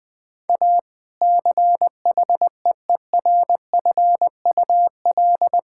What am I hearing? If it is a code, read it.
Morse code, heard as a keyed tone, about 20 wpm, 700 Hz. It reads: A CHEERFUL
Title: Difficult Sentences with Hints - 20wpm